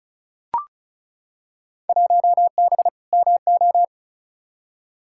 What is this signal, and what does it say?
Morse code, heard as a keyed tone, about 34 wpm, 700 Hz. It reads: 1B MO
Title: Field Day Exchanges - 35wpm